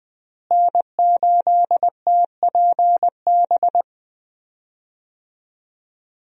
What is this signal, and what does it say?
Morse code, heard as a keyed tone, about 21 wpm, 700 Hz. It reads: N8TPB
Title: USA Call Signs - 5 Characters - Repeated 3x - 20wpm